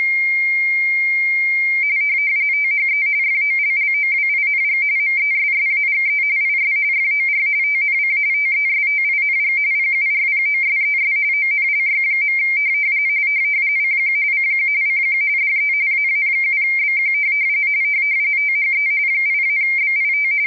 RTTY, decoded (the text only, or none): THE FOLLOWING MESSAGE IS FOR TNC TESTING PURPOSES ONLY. CQ CQ CQ DE AA7BQ THE 8-BIT